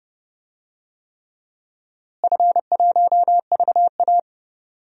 Morse code, heard as a keyed tone, about 30 wpm, 700 Hz. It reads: F1VA
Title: DXCC-100 Call Signs - 4 Characters - Repeated 3x - 30wpm